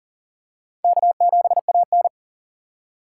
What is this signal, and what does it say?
Morse code, heard as a keyed tone, about 40 wpm, 700 Hz. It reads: K7AN
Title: USA Call Signs - 4 Characters - 3x Character Spacing - 40wpm